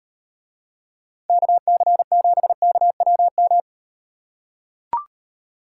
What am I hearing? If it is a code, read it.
Morse code, heard as a keyed tone, about 38 wpm, 700 Hz. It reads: KC7KWM